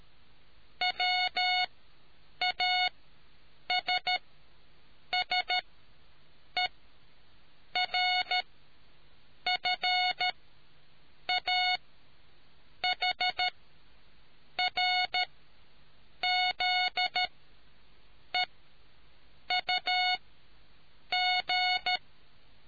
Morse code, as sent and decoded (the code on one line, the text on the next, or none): .-- .- ... ... .-. ..-. .- .... .-. --.. ..- --.
WASSERFAHRZEUG